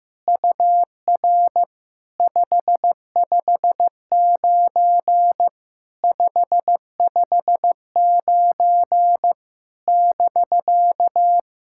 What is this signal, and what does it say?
Morse code, heard as a keyed tone, about 15 wpm, 690 Hz. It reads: UR 559 559 <BK>
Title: SOTA QSO Practice 15wpm